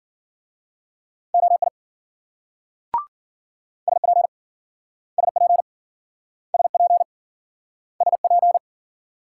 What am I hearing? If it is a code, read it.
Morse code, heard as a keyed tone, about 50 wpm, 700 Hz. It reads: OI SP SP SP SP